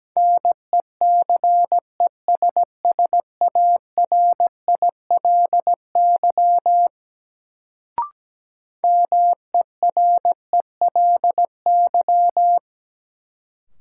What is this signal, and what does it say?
Morse code, heard as a keyed tone, about 17 wpm, 700 Hz. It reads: NECESSARILY MERELY
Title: Sentences from Top 1400 Words - New Word Drill Alt - 17wpm